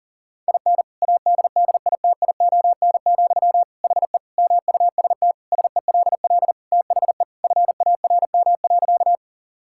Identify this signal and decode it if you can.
Morse code, heard as a keyed tone, about 40 wpm, 700 Hz. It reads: IN ADDITION, HE MUST SELL THE FARM.
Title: Sentences from Top 1000 Words - Review Part 6 - 40wpm